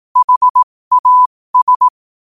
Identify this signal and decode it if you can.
Morse code, heard as a keyed tone, about 18 wpm, spaced as for 15 wpm, 1000 Hz. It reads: HAS
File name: qotdfiles / 15wpm